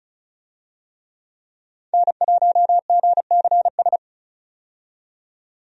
Morse code, heard as a keyed tone, about 36 wpm, 700 Hz. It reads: N1GCS
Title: USA Call Signs - 5 Characters - 35wpm